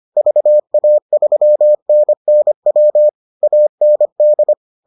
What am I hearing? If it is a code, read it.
Morse code, heard as a keyed tone, about 25 wpm, 600 Hz. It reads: VA3NNW AND